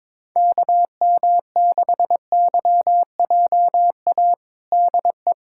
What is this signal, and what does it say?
Morse code, heard as a keyed tone, about 22 wpm, 700 Hz. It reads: KM6YJA DE
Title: USA Callsign DE Callsign - 6 Characters - 22wpm